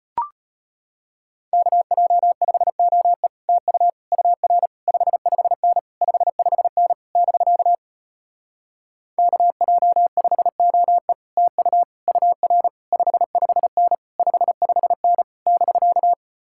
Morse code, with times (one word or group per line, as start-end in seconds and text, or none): 1.5-3.3 KJ5OE
3.5-3.9 TU
4.1-4.7 UR
4.9-5.8 55N
6.0-6.9 55N
7.2-7.8 <BK>
9.2-11.1 KJ5OE
11.4-11.8 TU
12.1-12.7 UR
12.9-13.9 55N
14.2-15.2 55N
15.5-16.1 <BK>